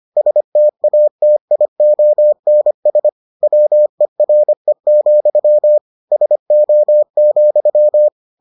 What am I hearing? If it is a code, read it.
Morse code, heard as a keyed tone, about 25 wpm, 600 Hz. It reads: STATIONS WERE, SO,